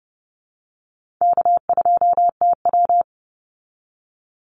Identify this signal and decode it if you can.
Morse code, heard as a keyed tone, about 30 wpm, 700 Hz. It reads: K2TW
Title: Easy Call Signs Repeated 3x - 30wpm